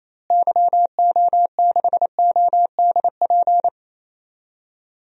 Morse code, heard as a keyed tone, about 28 wpm, 700 Hz. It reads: YO6ODP
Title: DXCC-100 Call Signs - 6 Characters - Repeated 2x - 28wpm